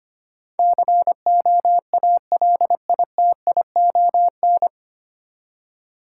Morse code, heard as a keyed tone, about 25 wpm, 690 Hz. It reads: COALITION
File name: practice / 25wpm / 9 Letter Words - First 3 Letters Given - 25wpm